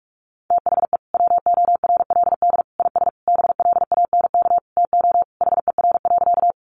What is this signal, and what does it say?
Morse code, heard as a keyed tone, about 44 wpm, 700 Hz. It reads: THE WORLD IS BLANK TO HER.
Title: Sentences from Top 1500 Words 45wpm